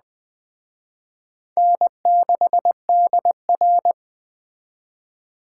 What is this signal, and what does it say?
Morse code, heard as a keyed tone, about 19 wpm, 700 Hz. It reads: N6DR